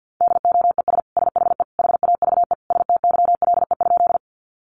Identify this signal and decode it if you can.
Morse code, heard as a keyed tone, about 50 wpm, 700 Hz. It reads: DOES SHE HAVE STYLE?
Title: Sentences from Top 800 Words 50wpm